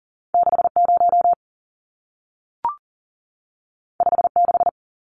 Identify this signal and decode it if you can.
Morse code, heard as a keyed tone, about 40 wpm, 700 Hz. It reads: 60 56
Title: Numbers - 2-Digit 40wpm